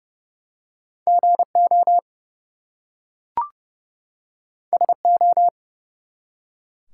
Morse code, heard as a keyed tone, about 30 wpm, 700 Hz. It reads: GO SO
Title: Two Letter Words - 30wpm